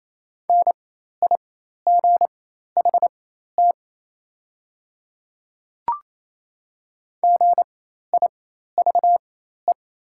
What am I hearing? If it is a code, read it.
Morse code, heard as a keyed tone, about 28 wpm, spaced as for 13 wpm, 700 Hz. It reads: NIGHT GIVE
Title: 4x Character Spacing - Top 300 Words - 28wpm